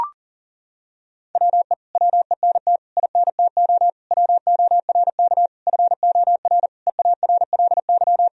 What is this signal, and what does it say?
Morse code, heard as a keyed tone, about 40 wpm, 700 Hz. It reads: WE WENT INTO WORK FOR EARLY
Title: Sentences from Top 1000 Words - Review Part 4 - 40wpm